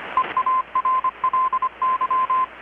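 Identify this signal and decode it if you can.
Morse code, heard as a keyed tone, about 25 wpm, 1040 Hz, with radio noise. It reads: EARLY
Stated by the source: Morse from KPH's Night of Nights XX maritime radiotelegraphy event, received in Chiapas, Mexico about 3,800 km away on a Kenwood R-600 with a grounded 10 m long-wire antenna.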